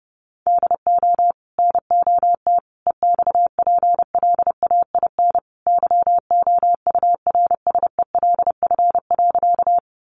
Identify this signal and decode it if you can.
Morse code, heard as a keyed tone, about 30 wpm, 700 Hz. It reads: DO NOT EXPLAIN YOURSELF.
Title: Sentences from Top 1100 Words 30wpm